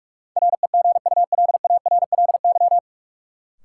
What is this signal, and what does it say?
Morse code, heard as a keyed tone, about 46 wpm, 700 Hz. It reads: REGULARLY
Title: Sentences from Top 1400 Words - New Word Drill - 45wpm